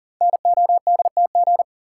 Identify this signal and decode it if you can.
Morse code, heard as a keyed tone, about 38 wpm, 700 Hz. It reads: NODTG
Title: Code Groups - 5 Letters - 40wpm